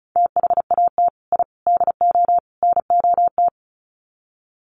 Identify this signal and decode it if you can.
Morse code, heard as a keyed tone, about 34 wpm, 700 Hz. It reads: THAT I DO NOT
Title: Sets of 4 Words 35wpm